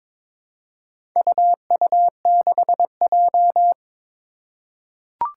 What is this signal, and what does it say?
Morse code, heard as a keyed tone, about 22 wpm, 700 Hz. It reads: UU6J
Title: Four Challenging Letter-Number Combinations 22wpm